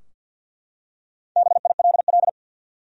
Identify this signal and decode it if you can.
Morse code, heard as a keyed tone, about 52 wpm, 700 Hz. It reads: BILL
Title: USA Common Names - Max 4 Letters - 50wpm